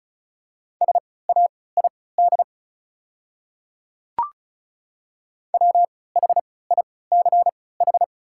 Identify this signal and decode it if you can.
Morse code, heard as a keyed tone, about 36 wpm, spaced as for 20 wpm, 690 Hz. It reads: SAID WHICH
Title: 3x Character Spacing - Top 100 Words - 35wpm